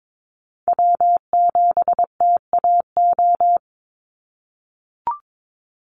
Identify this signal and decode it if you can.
Morse code, heard as a keyed tone, about 22 wpm, 700 Hz. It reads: W7TAO
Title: NA SOTA Chaser Callsigns 22wpm